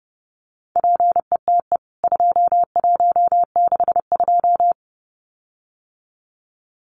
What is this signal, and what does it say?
Morse code, heard as a keyed tone, about 30 wpm, 700 Hz. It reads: PETE 2162
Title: CWT Exchange 30wpm